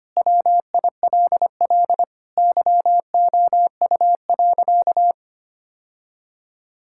Morse code, heard as a keyed tone, about 25 wpm, 700 Hz. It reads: WILL YOU.
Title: Sentences from Top 1000 Words - Review Part 6 - 25wpm